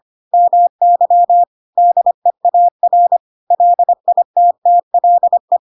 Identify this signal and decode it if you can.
Morse code, heard as a keyed tone, about 25 wpm, 700 Hz. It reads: MY DEAR LITTLE
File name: Pinocchio / Pinocchio ch7 25 25 ik1hge